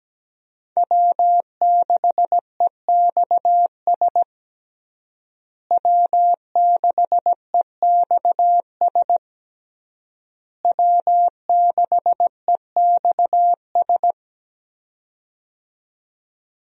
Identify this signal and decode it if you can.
Morse code, heard as a keyed tone, about 17 wpm, 700 Hz. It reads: W6EXS W6EXS W6EXS
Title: USA Call Signs - 5 Characters - Repeated 3x - 17wpm